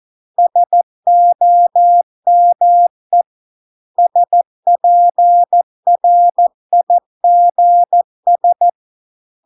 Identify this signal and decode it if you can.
Morse code, heard as a keyed tone, about 14 wpm, 700 Hz. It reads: SOME SPRIGS